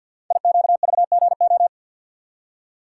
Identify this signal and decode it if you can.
Morse code, heard as a keyed tone, about 47 wpm, 700 Hz. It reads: IQVGO
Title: Code Groups - 5 Letters - 50wpm